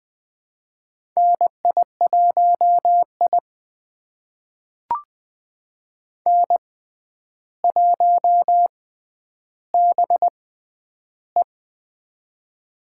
Morse code, heard as a keyed tone, about 20 wpm, spaced as for 10 wpm, 700 Hz. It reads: NI1I N1BE
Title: USA Call Signs - 4 Characters - 6x Character Spacing - 20wpm